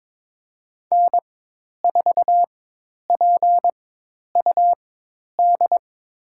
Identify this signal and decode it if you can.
Morse code, heard as a keyed tone, about 22 wpm, spaced as for 10 wpm, 700 Hz. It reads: N4PUD
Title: USA Call Signs - 5 Characters - 4x Character Spacing - 22wpm